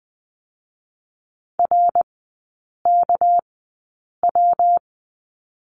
Morse code, heard as a keyed tone, about 20 wpm, spaced as for 8 wpm, 700 Hz. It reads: RKW
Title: 198 Lesson 32e - Letter k - Three Characters 20wpm